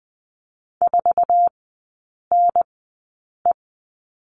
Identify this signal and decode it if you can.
Morse code, heard as a keyed tone, about 20 wpm, spaced as for 8 wpm, 700 Hz. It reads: 4NE
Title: 087 Lesson 15e - Number 5 - Three Characters 20wpm